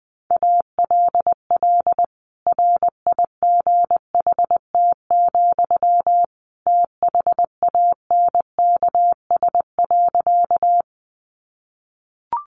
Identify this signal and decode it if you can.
Morse code, heard as a keyed tone, about 20 wpm, 700 Hz. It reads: ALL RIGHT, THANKS.
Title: Sentences from Top 600 Words 20wpm